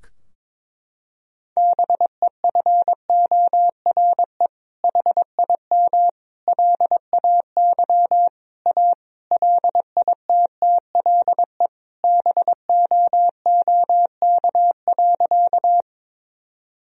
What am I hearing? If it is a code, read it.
Morse code, heard as a keyed tone, about 22 wpm, 700 Hz. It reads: BEFORE HIM LAY A LITTLE BOOK.
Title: Sentences from Top 1000 Words - Review Part 2 - 22wpm